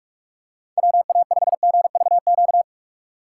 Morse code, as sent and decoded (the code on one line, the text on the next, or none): .-- .- ..... --. ...- --.-
WA5GVQ